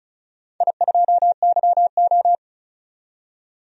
Morse code, heard as a keyed tone, about 36 wpm, 700 Hz. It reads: I2YO
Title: DXCC-100 Call Signs - 4 Characters - 35wpm